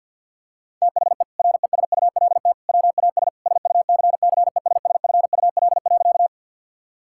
Morse code, heard as a keyed tone, about 50 wpm, 700 Hz. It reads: THE RESULT WAS SUCCESSFUL.
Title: Sentences from Top 1000 Words - Review Part 6 - 50wpm